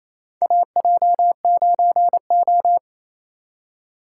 Morse code, heard as a keyed tone, about 28 wpm, 700 Hz. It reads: AJ9O
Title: USA Call Signs - 2x Character Spacing - 28wpm